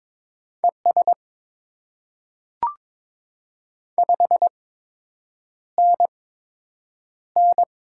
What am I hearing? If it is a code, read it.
Morse code, heard as a keyed tone, about 22 wpm, spaced as for 8 wpm, 700 Hz. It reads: ES 5NN
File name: Top 20 QSO Elements - 8x Character Spacing - 22wpm